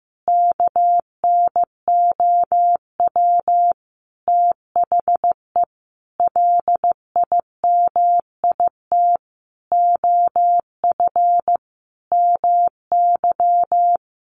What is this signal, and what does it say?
Morse code, heard as a keyed tone, about 15 wpm, 700 Hz. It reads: KNOW THE LIMIT OF MY
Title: Sentences from Top 1500 Words 15wpm